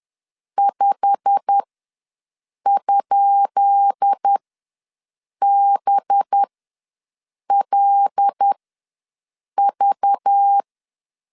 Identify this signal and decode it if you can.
Morse code, heard as a keyed tone, about 10 wpm, spaced as for 6 wpm, 800 Hz. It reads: 5?BLV
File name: LEZIONI / 10 SIGNS / 10B INT ws 10wpm 4134